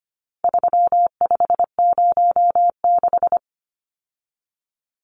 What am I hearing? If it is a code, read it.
Morse code, heard as a keyed tone, about 25 wpm, 700 Hz. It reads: 3506
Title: Numbers - 4-Digit 25wpm